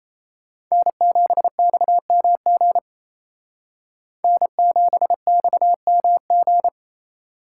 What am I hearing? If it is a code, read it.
Morse code, heard as a keyed tone, about 35 wpm, 700 Hz. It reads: N7XMG N7XMG